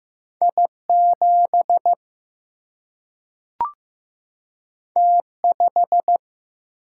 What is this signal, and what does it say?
Morse code, heard as a keyed tone, about 15 wpm, 700 Hz. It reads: I7 T5
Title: Single-Letter Single-Number 15wpm